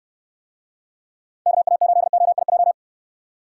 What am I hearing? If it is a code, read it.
Morse code, heard as a keyed tone, about 69 wpm, 700 Hz. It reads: KA8OIJ